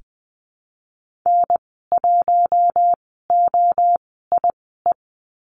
Morse code, heard as a keyed tone, about 20 wpm, spaced as for 14 wpm, 700 Hz. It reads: N1OIE
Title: 042 Lesson 8g - Number 1 - Call Signs 20wpm